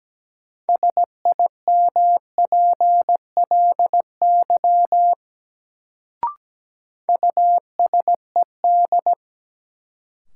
Morse code, heard as a keyed tone, about 17 wpm, 700 Hz. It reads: SIMPLY USED
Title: Sentences from Top 600 Words - New Word Drill - 17wpm